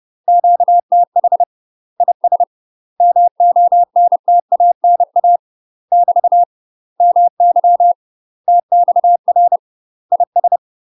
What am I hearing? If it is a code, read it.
Morse code, heard as a keyed tone, about 30 wpm, 700 Hz. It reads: QTH IS MONTANA <BT> MY TXR IS
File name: UK MASTER QSOs TRAINING / 30WPM / UK MASTER QSOs 0005